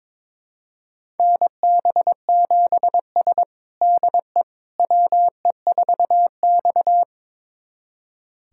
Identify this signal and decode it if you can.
Morse code, heard as a keyed tone, about 22 wpm, 700 Hz. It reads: NB7S DE WE4X
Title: USA Callsign DE Callsign - 4 Characters - 22wpm